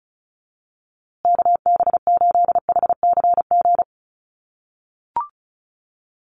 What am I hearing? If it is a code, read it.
Morse code, heard as a keyed tone, about 34 wpm, 700 Hz. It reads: KB8HCG
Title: USA Call Signs Repeated 2x 35wpm